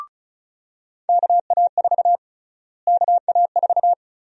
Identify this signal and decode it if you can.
Morse code, heard as a keyed tone, about 36 wpm, 700 Hz. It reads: KA4 KA4
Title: USA Call Sign Prefixes - Repeated 2x - 35wpm